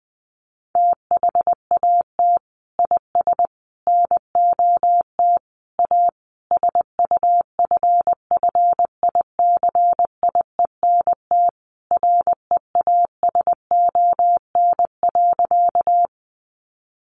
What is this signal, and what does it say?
Morse code, heard as a keyed tone, about 20 wpm, 700 Hz. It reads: THAT IS NOT A SUFFICIENT REASON.
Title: Sentences from Top 1400 Words 20wpm